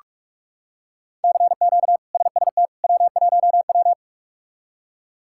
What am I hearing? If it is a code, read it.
Morse code, heard as a keyed tone, about 44 wpm, 700 Hz. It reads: CQ SST W1W
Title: K1USN SST QSO Practice - 45wpm